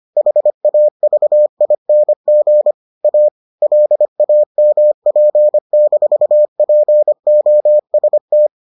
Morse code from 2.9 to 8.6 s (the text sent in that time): A LAMP-POST